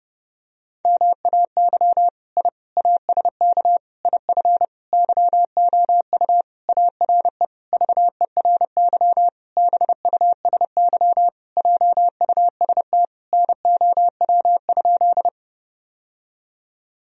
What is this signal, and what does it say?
Morse code, heard as a keyed tone, about 30 wpm, 700 Hz. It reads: MAY I ASK IF YOU ARE VERY BUSY JUST NOW?
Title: Sentences from Top 1000 Words - Review Part 3 - 30wpm